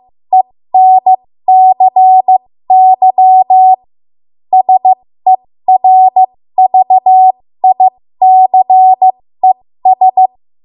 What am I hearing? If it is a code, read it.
Morse code, heard as a keyed tone, about 15 wpm, spaced as for 13 wpm, 750 Hz. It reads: ENCY SERVICES
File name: Morse / 13WPM